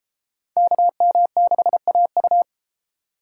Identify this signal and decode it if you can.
Morse code, heard as a keyed tone, about 30 wpm, 700 Hz. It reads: KM6AU